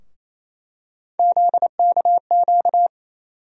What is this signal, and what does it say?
Morse code, heard as a keyed tone, about 28 wpm, 700 Hz. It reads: ZKQ
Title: USA Call Sign Postfixes - 3 Letters 28wpm